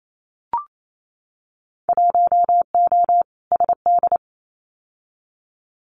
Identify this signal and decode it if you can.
Morse code, heard as a keyed tone, about 28 wpm, 700 Hz. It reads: 1O SD
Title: Winter Field Day Exchanges Encore 28wpm